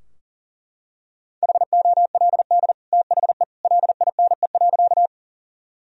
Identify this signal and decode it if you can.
Morse code, heard as a keyed tone, about 40 wpm, 700 Hz. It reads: HOLD THE LINE.
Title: Sentences from Top 1000 Words - Review Part 2 - 40wpm